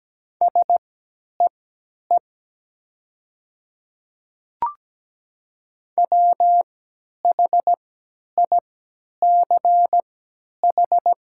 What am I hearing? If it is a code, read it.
Morse code, heard as a keyed tone, about 17 wpm, spaced as for 10 wpm, 700 Hz. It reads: SEE WHICH
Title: 3x Character Spacing - Top 200 Words - 17wpm